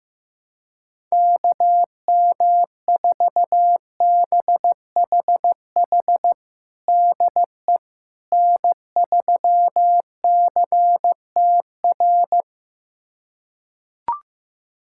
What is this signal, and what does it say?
Morse code, heard as a keyed tone, about 16 wpm, 700 Hz. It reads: KM4BHH DE N3CTR
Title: USA Callsign DE Callsign 15wpm